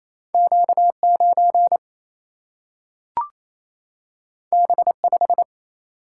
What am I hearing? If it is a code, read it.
Morse code, heard as a keyed tone, about 28 wpm, 700 Hz. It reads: Q9 B5
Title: Single-Letter Single-Number 28wpm